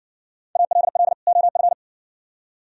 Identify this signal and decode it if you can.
Morse code, heard as a keyed tone, about 55 wpm, 700 Hz. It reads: ALL OF